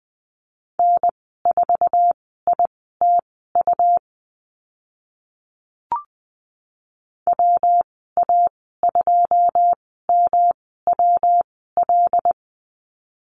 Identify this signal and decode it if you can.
Morse code, heard as a keyed tone, about 20 wpm, spaced as for 14 wpm, 700 Hz. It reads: N4ITU WA2MWL
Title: 140 Lesson 23g - Letter f - Call Signs 20wpm